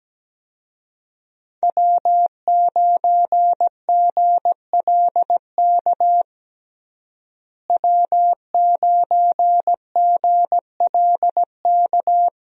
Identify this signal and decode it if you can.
Morse code, heard as a keyed tone, about 17 wpm, 700 Hz. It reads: W9GLK W9GLK
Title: USA Call Signs - 5 Characters - Repeated 2x - 17wpm